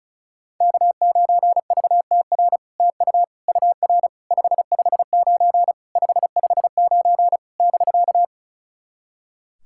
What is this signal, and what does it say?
Morse code, heard as a keyed tone, about 36 wpm, 700 Hz. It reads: K9VTR TU UR 559 559 <BK>
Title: SOTA QSO Practice 35wpm